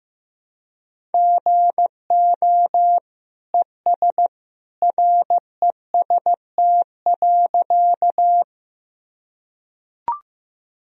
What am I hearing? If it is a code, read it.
Morse code, heard as a keyed tone, about 15 wpm, 700 Hz. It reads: GO ES REST.
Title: Sentences from Top 500 Words and Abbreviations 15wpm